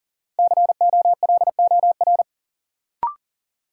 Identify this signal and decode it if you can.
Morse code, heard as a keyed tone, about 40 wpm, 700 Hz. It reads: COLOR
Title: Sentences from Top 1000 Words - Review Part 2 - New Word Drill - 40wpm